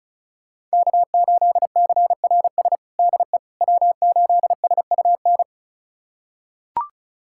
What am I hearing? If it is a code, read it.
Morse code, heard as a keyed tone, about 36 wpm, 700 Hz. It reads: K8CRS DE W8SUN
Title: USA Callsign DE Callsign - 5 Characters - 35wpm